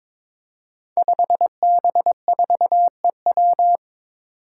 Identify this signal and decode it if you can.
Morse code, heard as a keyed tone, about 22 wpm, 700 Hz. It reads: 5B4EW
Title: Five Challenging Letter-Number Combinations 22wpm